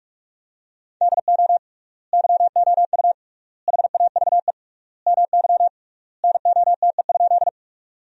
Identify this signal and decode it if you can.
Morse code, heard as a keyed tone, about 46 wpm, 700 Hz. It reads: DO YOU HAVE MY NOTE?
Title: 3x Word-Spacing - Sentences from Top 500 Words - 45wpm